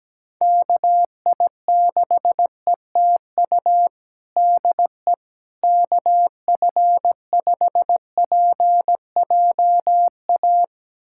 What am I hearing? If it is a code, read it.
Morse code, heard as a keyed tone, about 17 wpm, 700 Hz. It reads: KI6ETU DE KF5PJA
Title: USA Callsign DE Callsign - 6 Characters - 17wpm